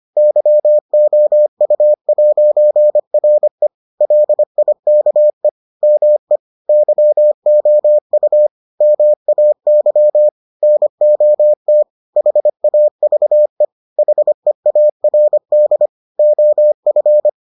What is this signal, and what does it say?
Morse code, heard as a keyed tone, about 25 wpm, 600 Hz. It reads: YOU'RE LIKE ME YOU MAY NOT HAVE HEARD OF